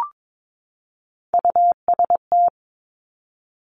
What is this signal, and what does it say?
Morse code, heard as a keyed tone, about 22 wpm, 700 Hz. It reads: UST
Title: Top Three Letter Combinations in Words 22wpm